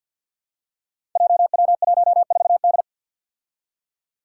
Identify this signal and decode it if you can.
Morse code, heard as a keyed tone, about 52 wpm, 700 Hz. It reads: JW14D
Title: Five Challenging Letter-Number Combinations 50wpm